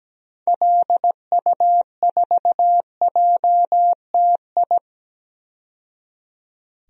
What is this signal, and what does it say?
Morse code, heard as a keyed tone, about 17 wpm, 700 Hz. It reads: LU4JTI
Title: DXCC-100 Call Signs - 6 Characters - 17wpm